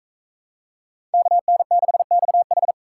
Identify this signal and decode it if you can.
Morse code, heard as a keyed tone, about 41 wpm, 700 Hz. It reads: KN6XH